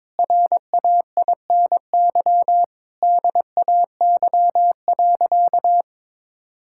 Morse code, heard as a keyed tone, about 22 wpm, 700 Hz. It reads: RAINY DAY.
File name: Difficult Sentences with Hints - 22wpm